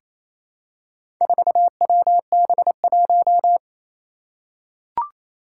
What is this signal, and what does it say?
Morse code, heard as a keyed tone, about 28 wpm, 700 Hz. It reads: 4WB1
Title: Four Challenging Letter-Number Combinations 28wpm